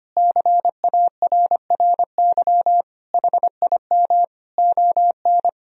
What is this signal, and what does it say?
Morse code, heard as a keyed tone, about 25 wpm, 700 Hz. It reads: CARRY HIM ON
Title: Sentences from Top 1000 Words - Review Part 1 - 25wpm